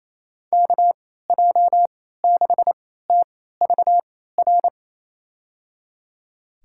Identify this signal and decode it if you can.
Morse code, heard as a keyed tone, about 28 wpm, spaced as for 16 wpm, 700 Hz. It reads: KJ6TVR
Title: USA Call Signs - 6 Characters - 3x Character Spacing - 28wpm